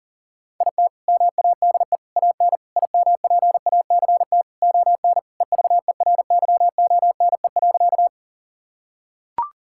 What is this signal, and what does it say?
Morse code, heard as a keyed tone, about 40 wpm, 700 Hz. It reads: IT MADE AN IMPACT ON EVERYONE.
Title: Sentences from Top 1000 Words - Review Part 6 - 40wpm